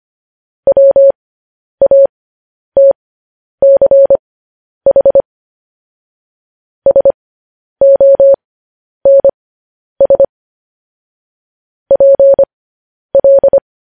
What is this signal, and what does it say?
Morse code, heard as a keyed tone, about 25 wpm, spaced as for 10 wpm, 550 Hz. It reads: WATCH SONS PL
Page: news headlines in morse code at 10 wpm